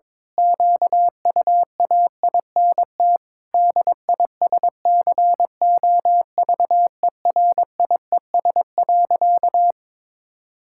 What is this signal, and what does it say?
Morse code, heard as a keyed tone, about 22 wpm, 700 Hz. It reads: QUAINT DISCOVERIES.
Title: Pangram Challenge v2 - 22wpm